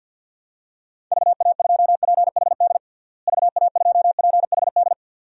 Morse code, heard as a keyed tone, about 52 wpm, 700 Hz. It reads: VA2PHD VA2PHD
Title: DXCC-100 Call Signs - 6 Characters - Repeated 2x - 50wpm